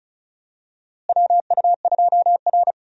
Morse code, heard as a keyed tone, about 34 wpm, 700 Hz. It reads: WU2R
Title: USA Call Signs - 4 Characters - Repeated 2x - 35wpm